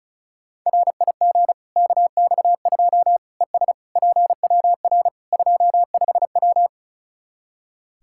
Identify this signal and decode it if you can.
Morse code, heard as a keyed tone, about 36 wpm, 700 Hz. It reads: RIG KX2 ES PWR 25W
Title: Practice QSO - Exchange 2 - Info - 35wpm